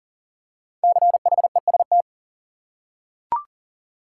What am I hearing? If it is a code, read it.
Morse code, heard as a keyed tone, about 40 wpm, 700 Hz. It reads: CHEST
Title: Five Letter Words - 40wpm